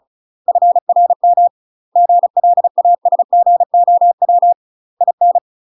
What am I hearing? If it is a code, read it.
Morse code, heard as a keyed tone, about 34 wpm, 700 Hz. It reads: FRM GLASGOW IN
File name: UK MASTER QSOs TRAINING / 35WPM / UK MASTER QSOs 0034